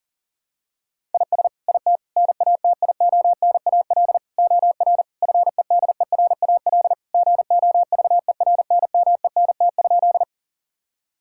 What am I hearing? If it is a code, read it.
Morse code, heard as a keyed tone, about 40 wpm, 700 Hz. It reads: IS IT NATIONAL OR FEDERAL GOVERNMENT?
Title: Sentences from Top 1000 Words - Review Part 5 - 40wpm